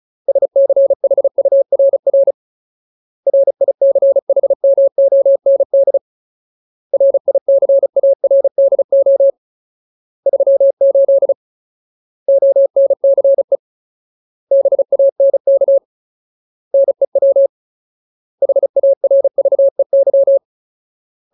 Morse code, heard as a keyed tone, about 34 wpm, 550 Hz. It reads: SCHURR RICHMOND RICARDO 38 ONCE BANK NEW HARVEY
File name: Ultimate Word List / 35WPM Ultimate